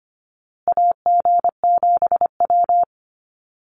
Morse code, heard as a keyed tone, about 25 wpm, 700 Hz. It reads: AG7W
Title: USA Call Signs Repeated 3x 25wpm